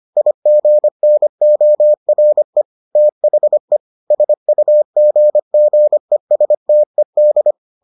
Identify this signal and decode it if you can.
Morse code, heard as a keyed tone, about 25 wpm, 600 Hz. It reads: IGNORE THE SUGGESTED